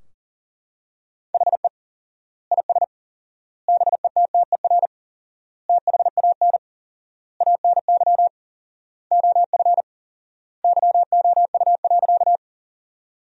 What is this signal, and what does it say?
Morse code, heard as a keyed tone, about 37 wpm, 700 Hz. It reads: HE IS BETTER THAN ANY OF YOU.
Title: 4x Word-Spacing - Sentences from Top 400 Words - 40wpm